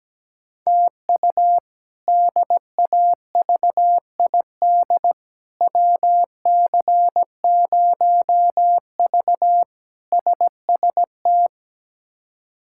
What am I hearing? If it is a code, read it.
Morse code, heard as a keyed tone, about 17 wpm, 700 Hz. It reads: TU DAVID WC0V SST
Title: K1USN SST QSO Practice - 17wpm